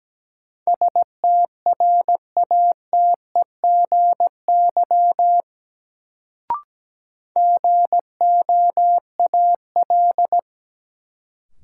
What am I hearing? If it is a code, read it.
Morse code, heard as a keyed tone, about 17 wpm, 700 Hz. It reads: STRATEGY GOAL
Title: Sentences from Top 1000 Words - Review Part 5 - New Word Drill - 17wpm